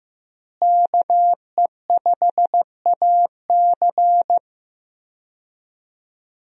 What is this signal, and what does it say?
Morse code, heard as a keyed tone, about 15 wpm, 700 Hz. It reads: KE5AC
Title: USA Call Signs - 5 Characters - Repeated 2x - 15wpm